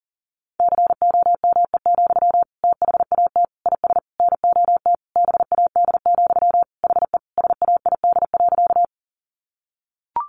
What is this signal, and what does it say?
Morse code, heard as a keyed tone, about 40 wpm, 700 Hz. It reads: COME, THAT IS NOT BAD, HE SAID.
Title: Sentences from Top 700 Words 40wpm